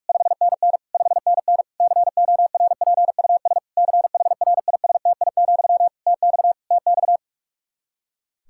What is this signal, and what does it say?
Morse code, heard as a keyed tone, about 46 wpm, 700 Hz. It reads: 5NN 5NN CORPUS CHRISTI, TX TX
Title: K1USN SST QSO Practice 45wpm